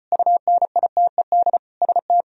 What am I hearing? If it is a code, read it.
Morse code, heard as a keyed tone, about 38 wpm, 700 Hz. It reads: UNITED ST